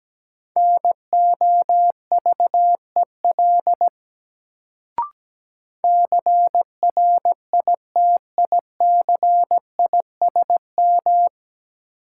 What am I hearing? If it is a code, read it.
Morse code, heard as a keyed tone, about 17 wpm, 700 Hz. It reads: NOVEL CRITICISM
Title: Sentences from Top 1300 Words - New Word Drill Alt - 17wpm